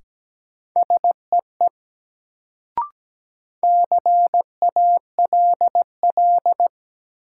Morse code, heard as a keyed tone, about 17 wpm, 700 Hz. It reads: SEE CALL
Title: Sentences from Top 100 Words - New Word Drill - 17wpm